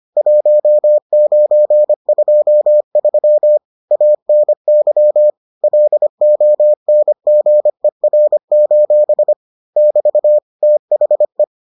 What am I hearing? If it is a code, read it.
Morse code, heard as a keyed tone, about 25 wpm, 600 Hz. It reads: 1923 ANY LONGER: <BT> THE